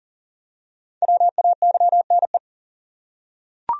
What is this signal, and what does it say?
Morse code, heard as a keyed tone, about 40 wpm, 700 Hz. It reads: WAYNE
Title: USA Common Names - 40wpm